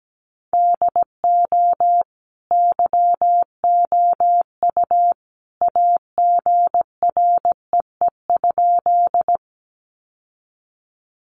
Morse code, heard as a keyed tone, about 17 wpm, 700 Hz. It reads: DO YOU AGREE?
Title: Sentences from Top 1100 Words 17wpm